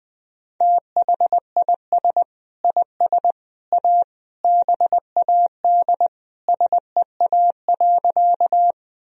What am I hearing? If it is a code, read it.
Morse code, heard as a keyed tone, about 20 wpm, 700 Hz. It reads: THIS IS A BAD SEA.
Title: Sentences from Top 1000 Words - Review Part 1 - 20wpm